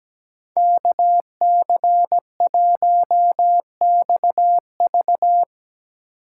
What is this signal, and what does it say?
Morse code, heard as a keyed tone, about 18 wpm, 700 Hz. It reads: KC1XV
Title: Speed Racing - USA 2x2 Call Signs 17wpm